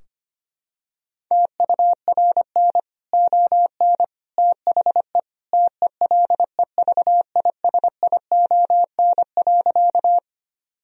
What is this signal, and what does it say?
Morse code, heard as a keyed tone, about 25 wpm, 700 Hz. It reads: TURN ON THE TELEVISION.
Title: Sentences from Top 1000 Words - Review Part 4 - 25wpm